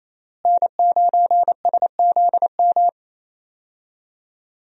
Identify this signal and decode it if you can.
Morse code, heard as a keyed tone, about 28 wpm, 700 Hz. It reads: N9SZM
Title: USA Call Signs - 5 Characters - Repeated 3x - 28wpm